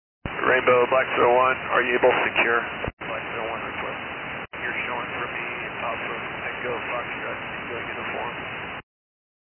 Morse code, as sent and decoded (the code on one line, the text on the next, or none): none